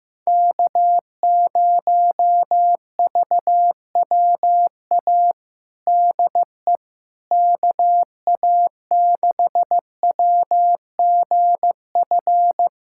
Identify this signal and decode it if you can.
Morse code, heard as a keyed tone, about 15 wpm, 700 Hz. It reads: K0VWA DE KA6WGF
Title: USA Callsign DE Callsign 15wpm